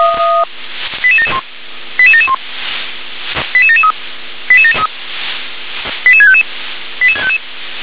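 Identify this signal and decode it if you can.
Five-tone selective call (ZVEI), heard as quick runs of five tones, tones of about 70 ms. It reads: 88771 88771 88773 88655
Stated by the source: ZVEI five-tone selective call received on an SDR.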